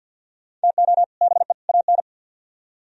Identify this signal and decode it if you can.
Morse code, heard as a keyed tone, about 62 wpm, 700 Hz. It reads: TO BE AN
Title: Speed-Racing - Sets of 3 Words 50wpm